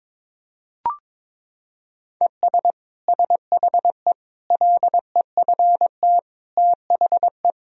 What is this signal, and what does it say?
Morse code, heard as a keyed tone, about 22 wpm, 700 Hz. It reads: ES SHE LEFT THE
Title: Sentences from Top 300 Words and Abbreviations 22wpm